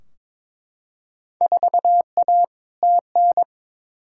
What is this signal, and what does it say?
Morse code, heard as a keyed tone, about 22 wpm, 700 Hz. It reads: 4A TN
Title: Field Day Exchanges - 22wpm